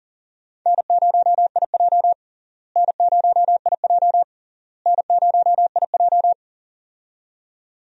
Morse code, heard as a keyed tone, about 40 wpm, 700 Hz. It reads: N0IJ N0IJ N0IJ
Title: USA Call Signs - 4 Characters - Repeated 3x - 40wpm